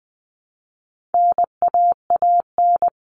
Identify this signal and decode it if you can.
Morse code, heard as a keyed tone, about 20 wpm, 700 Hz. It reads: NAAN
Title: All Cut Numbers - 4-Digit 20wpm